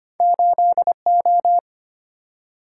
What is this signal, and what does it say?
Morse code, heard as a keyed tone, about 25 wpm, 700 Hz. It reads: 8O
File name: Two Letter-Number Combinations - 25wpm